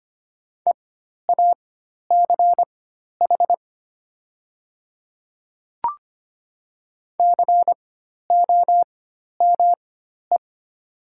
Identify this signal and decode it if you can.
Morse code, heard as a keyed tone, about 25 wpm, spaced as for 12 wpm, 700 Hz. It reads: EACH COME
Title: 4x Character Spacing - Top 100 Words - 25wpm